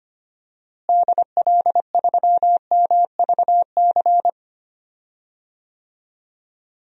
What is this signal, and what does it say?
Morse code, heard as a keyed tone, about 25 wpm, 700 Hz. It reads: DL3MVC
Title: DXCC-100 Call Signs - 6 Characters - Repeated 3x - 25wpm